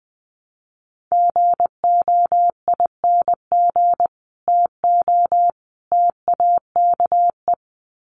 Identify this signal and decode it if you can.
Morse code, heard as a keyed tone, about 20 wpm, 700 Hz. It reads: GOING TO TAKE
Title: Sets of 3 Words 20wpm